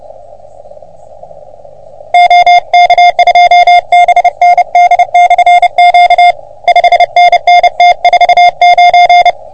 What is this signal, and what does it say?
Morse code, heard as a keyed tone, about 29 wpm, 720 Hz. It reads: OK2BND/Q 5NN T49
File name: OM3PA PILUP CUC 388